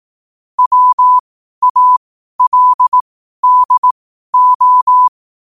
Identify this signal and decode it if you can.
Morse code, heard as a keyed tone, about 18 wpm, spaced as for 12 wpm, 1000 Hz. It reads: WALDO
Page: New quotes every day in morse code at 12 Words per minute.